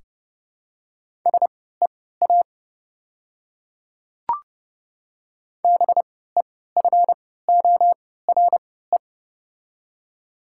Morse code, heard as a keyed tone, about 30 wpm, spaced as for 17 wpm, 700 Hz. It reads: SEA BEFORE
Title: 3x Character Spacing - Top 500 Words - 30wpm